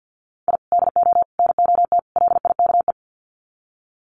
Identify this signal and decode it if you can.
Morse code, heard as a keyed tone, about 50 wpm, 700 Hz. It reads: I DO NOT LIKE
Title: Sets of 4 Words 50wpm